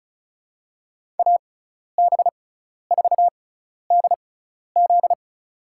Morse code, heard as a keyed tone, about 36 wpm, spaced as for 12 wpm, 700 Hz. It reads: AB4DZ
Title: USA Call Signs - 5 Characters - 6x Character Spacing - 35wpm